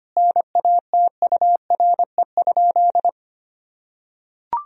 Morse code, heard as a keyed tone, about 25 wpm, 700 Hz. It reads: NATURE?
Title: Sentences from Top 1000 Words - Review Part 3 - 25wpm